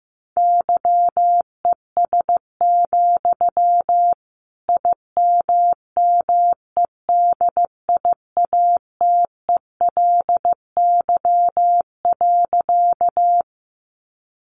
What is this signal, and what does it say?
Morse code, heard as a keyed tone, about 15 wpm, 700 Hz. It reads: YES, IMMEDIATELY.
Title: Sentences from Top 1100 Words 15wpm